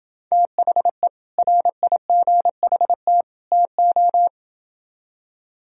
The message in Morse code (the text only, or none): THE RIGHT TO